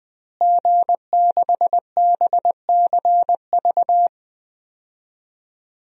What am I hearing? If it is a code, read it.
Morse code, heard as a keyed tone, about 20 wpm, 700 Hz. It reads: G6BCV
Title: DXCC-100 Call Signs - 5 Characters - Repeated 3x - 20wpm